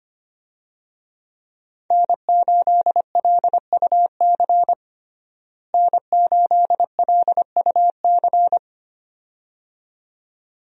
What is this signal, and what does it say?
Morse code, heard as a keyed tone, about 25 wpm, 700 Hz. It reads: N8LUC N8LUC
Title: USA Call Signs - 5 Characters - Repeated 2x - 25wpm